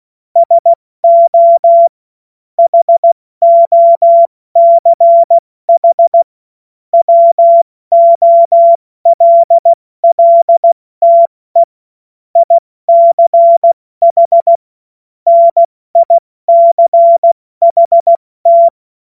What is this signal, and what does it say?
Morse code, heard as a keyed tone, about 16 wpm, 680 Hz. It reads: SO HOCH WOLLTE ICH NICHT
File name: Witz-26-August-2025-70-F80